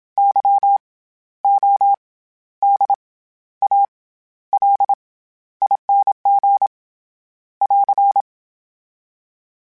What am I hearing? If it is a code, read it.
Morse code, heard as a keyed tone, about 27 wpm, spaced as for 13 wpm, 800 Hz. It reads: YODALING <AR>
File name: Special INGs-mp3s / Special INGs-mp3s / SlowToFaster ing @25